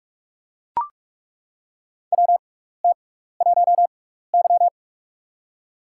Morse code, heard as a keyed tone, about 41 wpm, spaced as for 15 wpm, 700 Hz. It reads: WT1Y
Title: USA Call Signs - 4 Characters - 6x Character Spacing - 45wpm